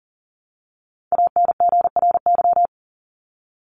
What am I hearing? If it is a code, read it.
Morse code, heard as a keyed tone, about 40 wpm, 700 Hz. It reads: ANGRY
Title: Top 1751-2000 Words 40wpm